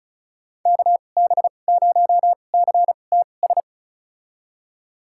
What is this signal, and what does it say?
Morse code, heard as a keyed tone, about 36 wpm, spaced as for 25 wpm, 700 Hz. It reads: KB0CTS
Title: USA Call Signs - 6 Characters - 2x Character Spacing - 35wpm